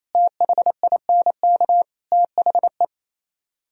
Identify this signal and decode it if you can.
Morse code, heard as a keyed tone, about 37 wpm, 700 Hz. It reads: THINK THE